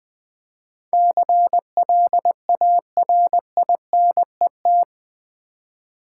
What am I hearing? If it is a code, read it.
Morse code, heard as a keyed tone, about 21 wpm, 700 Hz. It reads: CLARINET
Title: Easy Words - Speed-Racing with Hints - 20wpm